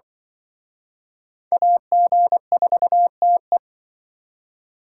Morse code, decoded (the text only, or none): AG4TE